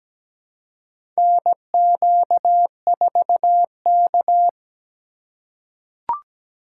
Morse code, heard as a keyed tone, about 17 wpm, 700 Hz. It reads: NQ4K
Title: USA Call Signs - 4 Characters - 3x Character Spacing - 17wpm